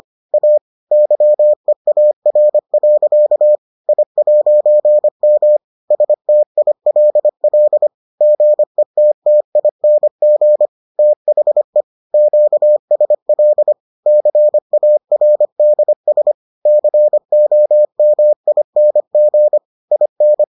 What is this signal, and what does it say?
Morse code, heard as a keyed tone, about 25 wpm, 600 Hz. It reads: A YEAR. I'M STILL GETTING THE QSL CARDS COMING IN